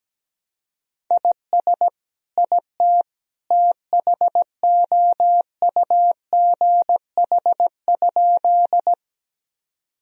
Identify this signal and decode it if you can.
Morse code, heard as a keyed tone, about 17 wpm, 700 Hz. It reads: IS IT THOUGH?
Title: Sentences from Top 1000 Words - Review Part 2 - 17wpm